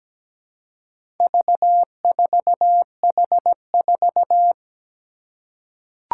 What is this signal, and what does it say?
Morse code, heard as a keyed tone, about 17 wpm, 700 Hz. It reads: V4H4
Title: Four Challenging Letter-Number Combinations 17wpm